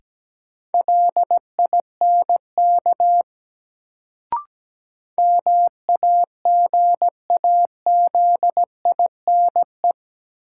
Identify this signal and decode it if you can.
Morse code, heard as a keyed tone, about 17 wpm, 700 Hz. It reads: LINK MAGAZINE
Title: Sentences from Top 1000 Words - Review Part 7 - New Word Drill - 17wpm